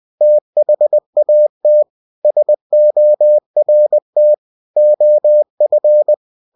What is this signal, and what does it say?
Morse code, heard as a keyed tone, about 20 wpm, 600 Hz. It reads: THAT SORT OF
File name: cw-books / winnie-the-pooh / winnie-20wpm-page-0063